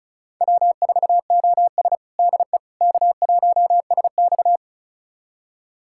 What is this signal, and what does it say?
Morse code, heard as a keyed tone, about 36 wpm, 700 Hz. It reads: W4OS DE K1SX
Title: USA Callsign DE Callsign - 4 Characters - 35wpm